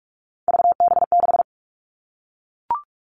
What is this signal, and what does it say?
Morse code, heard as a keyed tone, about 44 wpm, 700 Hz. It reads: VB6
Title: All Cut Numbers - 3-Digit 45wpm